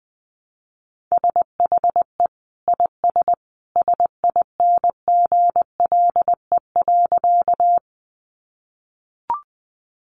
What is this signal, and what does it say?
Morse code, heard as a keyed tone, about 20 wpm, 700 Hz. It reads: SHE IS SINGLE.
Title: Sentences from Top 700 Words 20wpm